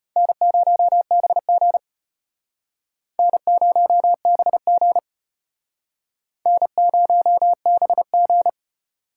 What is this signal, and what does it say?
Morse code, heard as a keyed tone, about 32 wpm, 700 Hz. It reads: N0BG N0BG N0BG